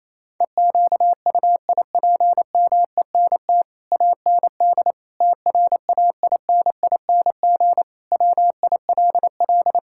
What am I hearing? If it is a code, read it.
Morse code, heard as a keyed tone, about 28 wpm, 700 Hz. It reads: EQUIPMENT AND TRAINING WILL
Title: Sentences from Top 1000 Words - Review Part 5 - 28wpm